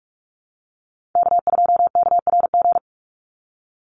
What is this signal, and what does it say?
Morse code, heard as a keyed tone, about 44 wpm, 700 Hz. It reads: K2KRG
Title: Call Signs - 45wpm